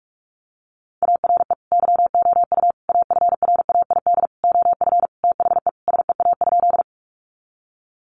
Morse code, heard as a keyed tone, about 44 wpm, 700 Hz. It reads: ARE YOU AFRAID OF THE SEA?
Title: Sentences from Top 1500 Words 45wpm